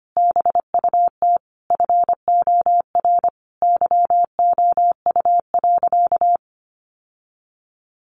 Morse code, heard as a keyed tone, about 25 wpm, 700 Hz. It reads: BUT FOR YOU.
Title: Sentences from Top 1300 Words 25wpm